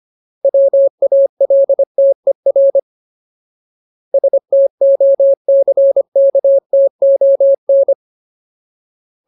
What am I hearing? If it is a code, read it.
Morse code, heard as a keyed tone, about 25 wpm, 550 Hz. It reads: WALTER STOCKTON
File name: Ultimate Word List / 25WPM Ultimate